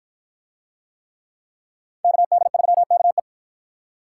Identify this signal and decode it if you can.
Morse code, heard as a keyed tone, about 53 wpm, 700 Hz. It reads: KD3KE